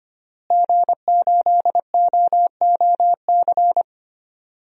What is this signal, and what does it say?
Morse code, heard as a keyed tone, about 25 wpm, 700 Hz. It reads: G8OOC
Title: DXCC-100 Call Signs - 5 Characters - Repeated 3x - 25wpm